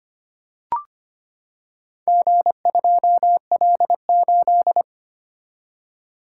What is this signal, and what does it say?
Morse code, heard as a keyed tone, about 25 wpm, 700 Hz. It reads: G2L8
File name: practice / 25wpm / Four Letter-Number Combinations - 25wpm